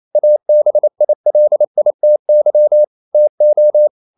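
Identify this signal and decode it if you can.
Morse code, heard as a keyed tone, about 28 wpm, 600 Hz. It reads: ABILITY TO
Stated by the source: Morse code practice at 28 wpm, text of Ken Follett's novel Never in English, from CW med Gnister.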